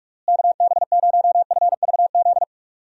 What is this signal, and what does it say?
Morse code, heard as a keyed tone, about 46 wpm, 700 Hz. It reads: KB0FVZ
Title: USA Call Signs - 6 Characters - Repeated 3x - 45wpm